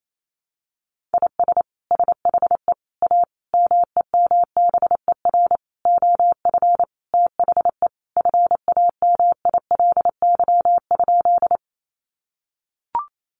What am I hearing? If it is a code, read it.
Morse code, heard as a keyed tone, about 28 wpm, 700 Hz. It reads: IS SHE A MEMBER OF THE FAMILY?
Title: Sentences from Top 1100 Words 28wpm